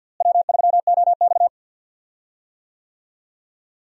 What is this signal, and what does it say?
Morse code, heard as a keyed tone, about 52 wpm, 700 Hz. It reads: W3OX
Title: USA Call Signs - 4 Characters - Repeated 2x - 50wpm